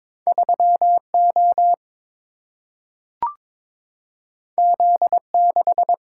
Morse code, heard as a keyed tone, about 22 wpm, 700 Hz. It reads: 3 O Z 6
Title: Single-Letter Single-Number 22wpm